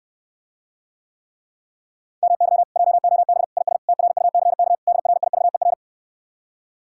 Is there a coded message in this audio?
Yes: Morse code from 2.2 to 5.7 s, heard as a keyed tone, about 69 wpm, 700 Hz. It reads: MY JOB IS TRUCK DRIVER